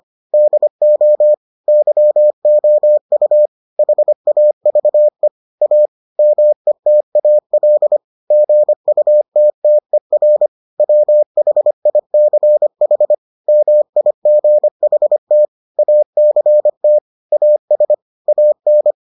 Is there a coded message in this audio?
Yes: Morse code from 0.3 to 18.9 s, heard as a keyed tone, about 25 wpm, 600 Hz. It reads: DO YOU HAVE A METAL GUTTER WHICH MIGHT ACT AS AN